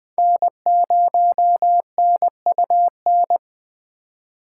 Morse code, heard as a keyed tone, about 22 wpm, 700 Hz. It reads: N0NUN